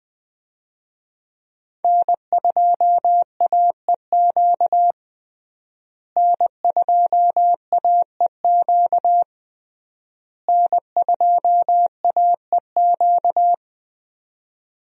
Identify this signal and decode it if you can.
Morse code, heard as a keyed tone, about 20 wpm, 700 Hz. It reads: N2AEQ N2AEQ N2AEQ
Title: USA Call Signs - 5 Characters - Repeated 3x - 20wpm